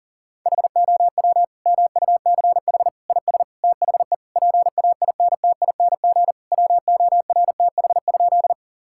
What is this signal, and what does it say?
Morse code, heard as a keyed tone, about 37 wpm, 700 Hz. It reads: HOW MUCH IS THE PAINTING WORTH?
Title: Sentences from Top 1000 Words - Review Part 6 - 40wpm